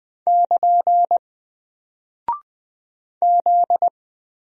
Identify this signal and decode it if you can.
Morse code, heard as a keyed tone, about 20 wpm, 700 Hz. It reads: <KN> Z
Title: Single Character (Letter:Number:Punctuation:Prosign) 20wpm